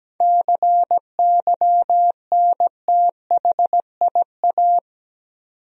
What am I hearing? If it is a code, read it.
Morse code, heard as a keyed tone, about 17 wpm, 700 Hz. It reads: CYNTHIA
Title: USA Common Names - 17wpm